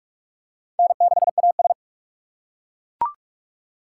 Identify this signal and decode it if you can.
Morse code, heard as a keyed tone, about 41 wpm, 700 Hz. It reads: N6AS
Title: USA Call Signs - 4 Characters - 3x Character Spacing - 45wpm